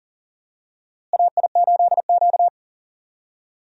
Morse code, heard as a keyed tone, about 39 wpm, 700 Hz. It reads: AI8Q
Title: USA Call Signs - 4 Characters - Repeated 3x - 40wpm